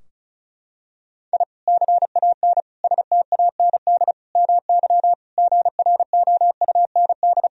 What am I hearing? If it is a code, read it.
Morse code, heard as a keyed tone, about 36 wpm, 700 Hz. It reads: I CAN STAND MY GROUND
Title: Sentences from Top 1000 Words - Review Part 2 - 35wpm